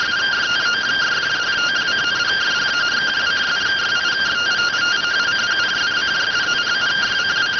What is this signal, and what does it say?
Two-tone teleprinter signal, about 170 Hz shift.